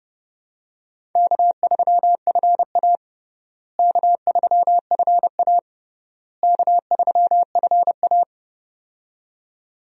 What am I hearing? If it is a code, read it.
Morse code, heard as a keyed tone, about 30 wpm, 700 Hz. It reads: K3FA K3FA K3FA
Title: USA Call Signs - 4 Characters - Repeated 3x - 30wpm